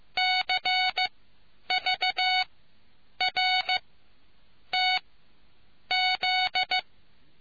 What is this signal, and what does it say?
Morse code, heard as a keyed tone, about 15 wpm, spaced as for 8 wpm, 740 Hz. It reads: CVRTZ